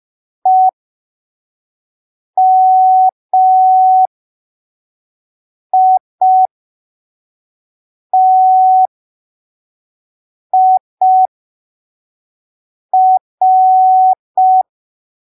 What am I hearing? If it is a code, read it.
Morse code, heard as a keyed tone, about 5 wpm, spaced as for 3 wpm, 750 Hz. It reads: EMITIR
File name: arquivos / texto / texto 16 05ppm